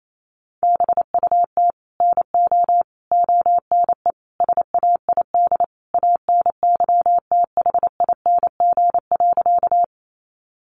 Morse code, heard as a keyed tone, about 28 wpm, 700 Hz. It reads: BUT NO ONE SAID ANYTHING.
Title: Sentences from Top 1000 Words 28wpm